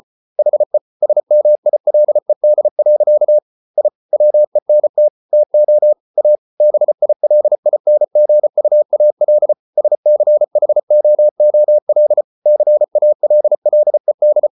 Morse code, heard as a keyed tone, about 34 wpm, 600 Hz. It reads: HE SMILED. I WENT TO A BILINGUAL SCHOOL CALLED